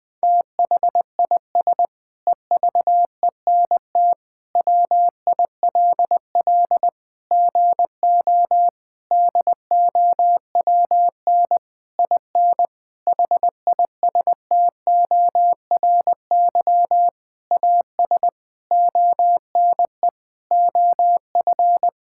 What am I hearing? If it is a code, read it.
Morse code, heard as a keyed tone, about 21 wpm, 700 Hz. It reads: THIS EVENT WILL GO DOWN IN HISTORY AS ONE OF
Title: Sentences with 9-Word Phrases - 20wpm